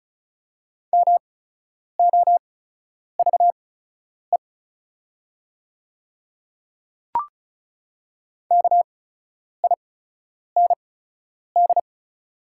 Morse code, heard as a keyed tone, about 36 wpm, spaced as for 10 wpm, 690 Hz. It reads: MOVE KIND
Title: 8x Character Spacing - Top 300 Words - 35wpm